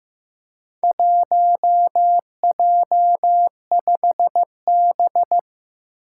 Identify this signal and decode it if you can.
Morse code, heard as a keyed tone, about 15 wpm, 700 Hz. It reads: 1J5B
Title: Four Challenging Letter-Number Combinations 15wpm